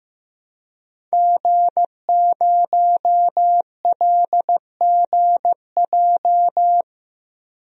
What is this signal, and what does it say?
Morse code, heard as a keyed tone, about 15 wpm, 700 Hz. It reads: G0LGJ
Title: DXCC-100 Call Signs - 5 Characters - Repeated 2x - 15wpm